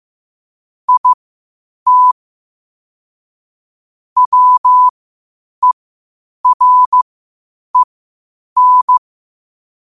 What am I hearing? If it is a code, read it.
Morse code, heard as a keyed tone, about 15 wpm, spaced as for 8 wpm, 1000 Hz. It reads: IT WEREN
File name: qotdfiles / 8wpm